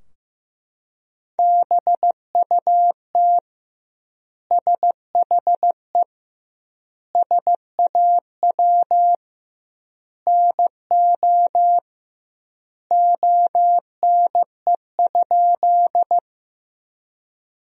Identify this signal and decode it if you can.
Morse code, heard as a keyed tone, about 15 wpm, 700 Hz. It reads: BUT SHE SAW NO ONE?
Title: 2x Word-Spacing - Sentences from Top 300 Words - 15wpm